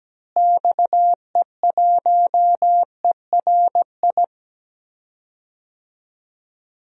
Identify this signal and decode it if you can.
Morse code, heard as a keyed tone, about 17 wpm, 700 Hz. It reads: XE1ERI
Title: DXCC-100 Call Signs - 6 Characters - Repeated 3x - 17wpm